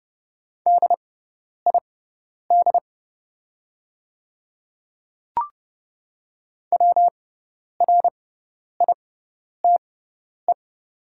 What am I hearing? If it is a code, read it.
Morse code, heard as a keyed tone, about 30 wpm, spaced as for 10 wpm, 700 Hz. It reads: DID WRITE
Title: 6x Character Spacing - Top 200 Words - 30wpm